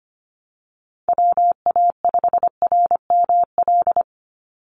Morse code, heard as a keyed tone, about 25 wpm, 700 Hz. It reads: WA5RML
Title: Call Signs - 25wpm